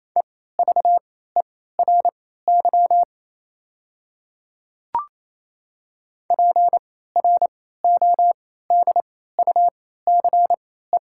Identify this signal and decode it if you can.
Morse code, heard as a keyed tone, about 28 wpm, spaced as for 16 wpm, 700 Hz. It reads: EVERY PRODUCE
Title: 3x Character Spacing - Top 500 Words - 28wpm